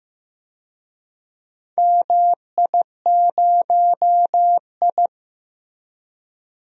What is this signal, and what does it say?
Morse code, heard as a keyed tone, about 15 wpm, 700 Hz. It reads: MI0I
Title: DXCC-100 Call Signs - 4 Characters - Repeated 2x - 15wpm